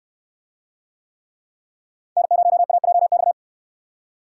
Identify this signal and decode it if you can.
Morse code, heard as a keyed tone, about 68 wpm, 700 Hz. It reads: N0AJX